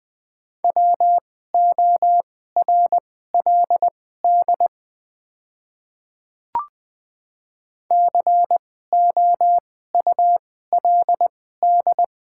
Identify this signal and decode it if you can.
Morse code, heard as a keyed tone, about 20 wpm, spaced as for 14 wpm, 700 Hz. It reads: WORLD COULD
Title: 2x Character Spacing - Top 300 Words - 20wpm